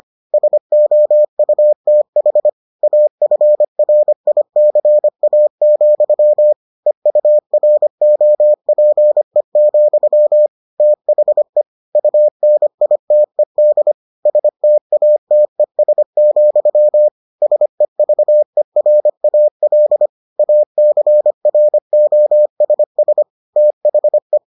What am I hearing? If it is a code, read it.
Morse code, heard as a keyed tone, about 25 wpm, 600 Hz. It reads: SOUTH AFRICA, EUROPE, THE UNITED STATES, SEVERAL ACROSS THE